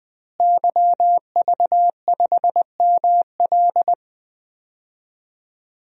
Morse code, heard as a keyed tone, about 20 wpm, 700 Hz. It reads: YV5ML
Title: DXCC-100 Call Signs - 5 Characters - Repeated 3x - 20wpm